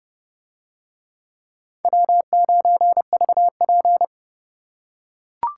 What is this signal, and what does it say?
Morse code, heard as a keyed tone, about 29 wpm, 700 Hz. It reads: W9VP